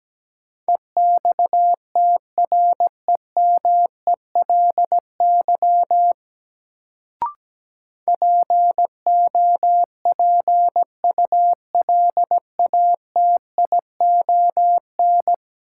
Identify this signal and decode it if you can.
Morse code, heard as a keyed tone, about 17 wpm, 700 Hz. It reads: EXTREMELY POPULATION
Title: Sentences from Top 1000 Words - Review Part 7 - New Word Drill - 17wpm